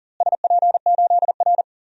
Morse code, heard as a keyed tone, about 39 wpm, 700 Hz. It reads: SP8R
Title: DXCC-100 Call Signs - 4 Characters - Repeated 3x - 40wpm